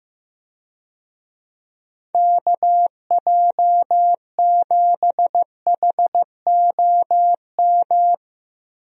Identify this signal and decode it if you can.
Morse code, heard as a keyed tone, about 15 wpm, 700 Hz. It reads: KJ7HOM
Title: USA Call Signs - 6 Characters - 15wpm